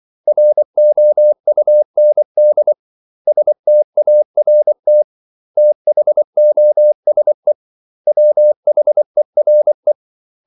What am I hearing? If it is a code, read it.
Morse code, heard as a keyed tone, about 24 wpm, 600 Hz. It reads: ROUND START THOSE WHERE
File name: CW / Practice / 500MostCommonEnglishWords24